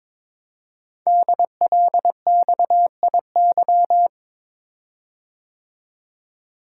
Morse code, heard as a keyed tone, about 22 wpm, 700 Hz. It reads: DLXIY
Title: Code Groups - 5 Letters - 22wpm